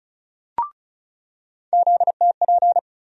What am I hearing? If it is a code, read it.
Morse code, heard as a keyed tone, about 36 wpm, 700 Hz. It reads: ZTP
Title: USA Call Sign Postfixes - 3 Letters 35wpm